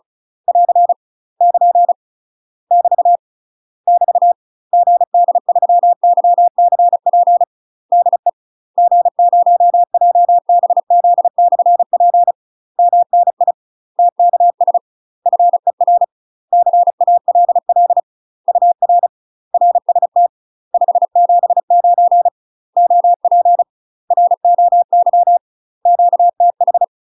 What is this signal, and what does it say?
Morse code, heard as a keyed tone, about 33 wpm, 700 Hz. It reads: <AR> <KN> <BT> <BT> GD3YCP DE G0JBZ/P MNI TKS FER CALL UR RST 579 OP ROY QTH
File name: UK MASTER QSOs TRAINING / 35WPM / UK MASTER QSOs 0022